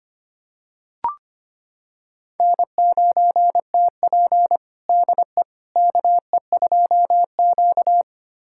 Morse code, heard as a keyed tone, about 25 wpm, 700 Hz. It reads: N9TP DE KE2Q
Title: USA Callsign DE Callsign - 4 Characters - 25wpm